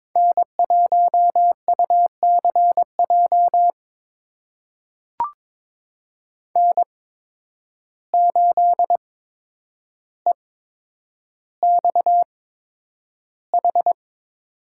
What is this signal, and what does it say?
Morse code, heard as a keyed tone, about 22 wpm, spaced as for 9 wpm, 700 Hz. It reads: N1UCJ N8EXH
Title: USA Call Signs - 5 Characters - 8x Character Spacing - 22wpm